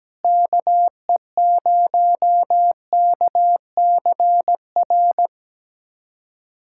Morse code, heard as a keyed tone, about 17 wpm, 700 Hz. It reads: KE0KCR
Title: USA Callsign DE Callsign - 6 Characters - 17wpm